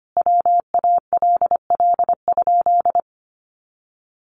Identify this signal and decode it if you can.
Morse code, heard as a keyed tone, about 25 wpm, 700 Hz. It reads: WALL?
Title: Sentences from Top 1100 Words 25wpm